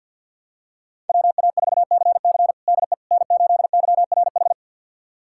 Morse code, heard as a keyed tone, about 53 wpm, 700 Hz. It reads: WA4CC DE N8YRH
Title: USA Callsign DE Callsign - 5 Characters - 50wpm